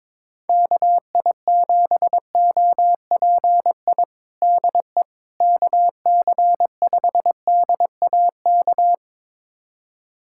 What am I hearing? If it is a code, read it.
Morse code, heard as a keyed tone, about 22 wpm, 700 Hz. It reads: KI7OPI DE KC5DAK
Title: USA Callsign DE Callsign 22wpm